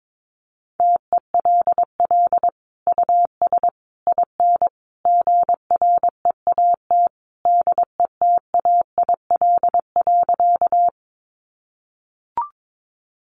Morse code, heard as a keyed tone, about 22 wpm, 700 Hz. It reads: TELL US IN GREAT DETAIL.
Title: Sentences from Top 1500 Words 22wpm